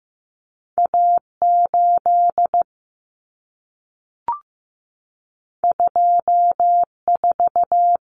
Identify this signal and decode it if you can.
Morse code, heard as a keyed tone, about 15 wpm, 700 Hz. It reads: A8 24
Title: Common Cut Numbers - 2-Digit 15wpm